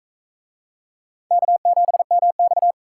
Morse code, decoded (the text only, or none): K7MX